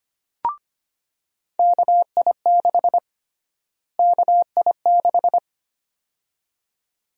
Morse code, heard as a keyed tone, about 25 wpm, 700 Hz. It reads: KI6 KI6
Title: USA Call Sign Prefixes - Repeated 2x - 25wpm